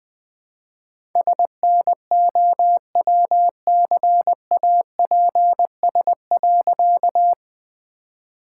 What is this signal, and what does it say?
Morse code, heard as a keyed tone, about 21 wpm, 700 Hz. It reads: SNOWCAPS.
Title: All-Age Appropriate Jokes - 20wpm